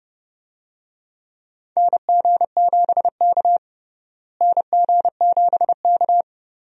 Morse code, heard as a keyed tone, about 30 wpm, 700 Hz. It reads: NG7K NG7K
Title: USA Call Signs - 4 Characters - Repeated 2x - 30wpm